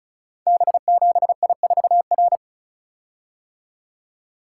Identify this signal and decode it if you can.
Morse code, heard as a keyed tone, about 36 wpm, 700 Hz. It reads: B7I4R
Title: Code Groups - 5 Letter:Numbers - 35wpm